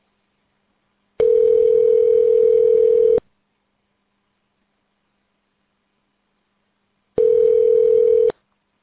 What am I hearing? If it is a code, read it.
Ringback tone (North America), 440 + 480 Hz.